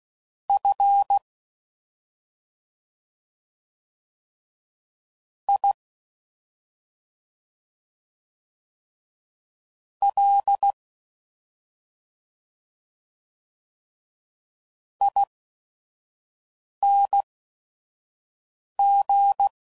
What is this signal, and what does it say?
Morse code, heard as a keyed tone, about 16 wpm, spaced as for 5 wpm, 800 Hz. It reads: F I L ING